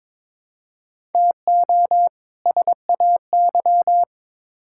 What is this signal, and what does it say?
Morse code, heard as a keyed tone, about 21 wpm, 700 Hz. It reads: TO SAY
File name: Build A Phrase - 2-Word Phrases - 22wpm